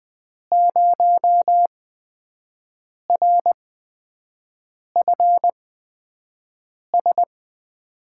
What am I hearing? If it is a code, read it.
Morse code, heard as a keyed tone, about 20 wpm, spaced as for 5 wpm, 700 Hz. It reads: 0RFS
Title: USA Call Signs - 5 Characters - 8x Character Spacing - 20wpm